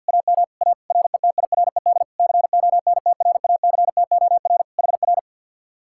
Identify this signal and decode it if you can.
Morse code, heard as a keyed tone, about 64 wpm, 700 Hz. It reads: AM A RETIRED CONTRACTOR HR